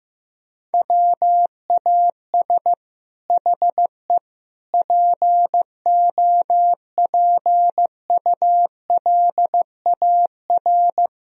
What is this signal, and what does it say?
Morse code, heard as a keyed tone, about 14 wpm, 700 Hz. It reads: WAS HE POPULAR
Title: Sentences from Top 1000 Words - Review Part 3 - 15wpm